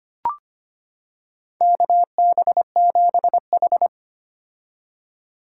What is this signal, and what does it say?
Morse code, heard as a keyed tone, about 25 wpm, 700 Hz. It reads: KB7H
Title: K1USN SST QSO Practice - 25wpm